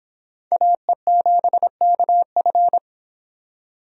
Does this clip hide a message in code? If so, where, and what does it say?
Yes: Morse code from 0.5 to 2.8 s, heard as a keyed tone, about 26 wpm, 700 Hz. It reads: AE7KF